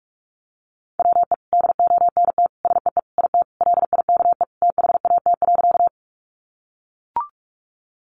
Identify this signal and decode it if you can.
Morse code, heard as a keyed tone, about 44 wpm, 700 Hz. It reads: WE DONT SEE IT LIKE THAT.
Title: Sentences from Top 300 Words 45wpm